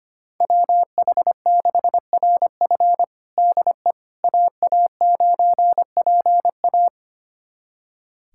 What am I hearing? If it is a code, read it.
Morse code, heard as a keyed tone, about 25 wpm, 700 Hz. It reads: WH6RF DE AA9PA
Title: USA Callsign DE Callsign - 5 Characters - 25wpm